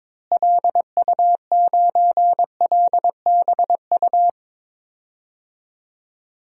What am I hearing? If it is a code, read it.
Morse code, heard as a keyed tone, about 22 wpm, 700 Hz. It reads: LU9LBU
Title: DXCC-100 Call Signs - 6 Characters - Repeated 2x - 22wpm